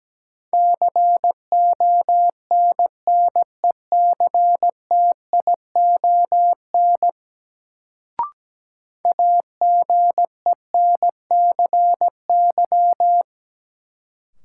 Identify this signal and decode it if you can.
Morse code, heard as a keyed tone, about 17 wpm, 700 Hz. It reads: CONNECTION AGENCY
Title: Sentences from Top 1300 Words - New Word Drill - 17wpm